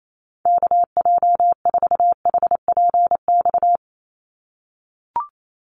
Morse code, heard as a keyed tone, about 28 wpm, 700 Hz. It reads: KJ4HPX
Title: USA Call Signs 28wpm